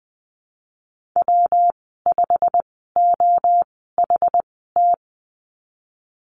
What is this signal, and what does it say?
Morse code, heard as a keyed tone, about 20 wpm, spaced as for 15 wpm, 700 Hz. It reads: W5OHT
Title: 186 Lesson 30g - Letter b - Call Signs 20wpm